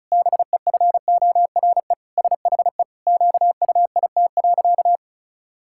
Morse code, heard as a keyed tone, about 34 wpm, 700 Hz. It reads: BEFORE SHE QUIT.
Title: Pangram Challenge v2 - 35wpm